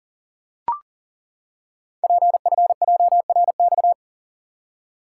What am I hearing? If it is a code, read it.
Morse code, heard as a keyed tone, about 38 wpm, 700 Hz. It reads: PFJRX
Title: Code Groups - 5 Letters - 40wpm